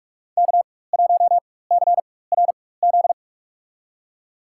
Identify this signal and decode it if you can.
Morse code, heard as a keyed tone, about 46 wpm, spaced as for 21 wpm, 690 Hz. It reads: K1CRZ
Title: USA Call Signs - 5 Characters - 4x Character Spacing - 45wpm